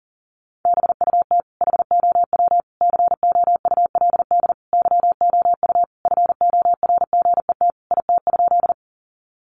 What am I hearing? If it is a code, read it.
Morse code, heard as a keyed tone, about 40 wpm, 700 Hz. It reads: BUT HOW COULD YOU FORGET IT?
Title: Sentences from Top 1200 Words 40wpm